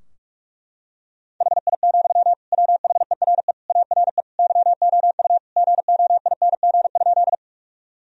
Morse code, heard as a keyed tone, about 42 wpm, 700 Hz. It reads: HI, WHERE ARE YOU GOING?
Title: Sentences from Top 1000 Words - Review Part 3 - 45wpm